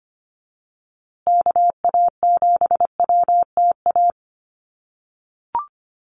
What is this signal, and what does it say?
Morse code, heard as a keyed tone, about 25 wpm, 700 Hz. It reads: KA7WTA
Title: USA Call Signs Repeated 3x 25wpm